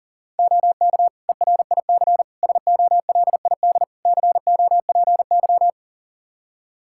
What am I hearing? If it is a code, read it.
Morse code, heard as a keyed tone, about 40 wpm, 700 Hz. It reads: OK ERIC SOLID COPY
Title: Practice QSO - Exchange 3 - More Info - 40wpm